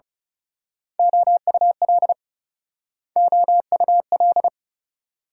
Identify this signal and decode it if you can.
Morse code, heard as a keyed tone, about 35 wpm, 700 Hz. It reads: OUL OUL
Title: Speed-Racing Multi-letter Phonemes - 30wpm